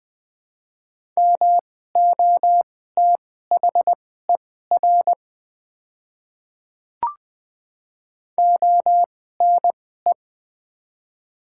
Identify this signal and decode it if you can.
Morse code, heard as a keyed tone, about 21 wpm, spaced as for 15 wpm, 700 Hz. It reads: MOTHER ONE
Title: 2x Character Spacing - Top 300 Words - 20wpm